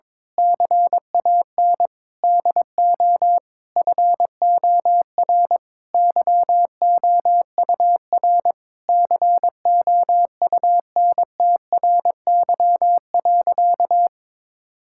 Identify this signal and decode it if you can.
Morse code, heard as a keyed tone, about 22 wpm, 700 Hz. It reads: CAN DO FOR YOUR COUNTRY.
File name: Sentences with 9-Word Phrases - 22wpm